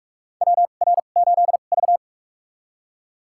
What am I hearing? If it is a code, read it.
Morse code, heard as a keyed tone, about 46 wpm, spaced as for 32 wpm, 700 Hz. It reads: WR8V
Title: USA Call Signs - 4 Characters - 2x Character Spacing - 45wpm